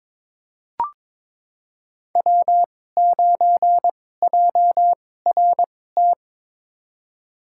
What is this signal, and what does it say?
Morse code, heard as a keyed tone, about 22 wpm, spaced as for 16 wpm, 690 Hz. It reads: W9JRT
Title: USA Call Signs - 5 Characters - 2x Character Spacing - 22wpm